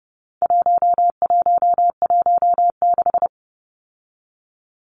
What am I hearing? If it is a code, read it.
Morse code, heard as a keyed tone, about 30 wpm, 700 Hz. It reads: 1116
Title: Numbers - 4-Digit 30wpm